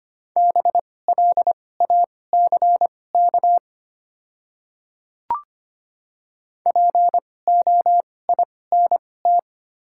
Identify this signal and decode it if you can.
Morse code, heard as a keyed tone, about 25 wpm, spaced as for 18 wpm, 700 Hz. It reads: BLACK POINT
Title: 2x Character Spacing - Top 500 Words - 25wpm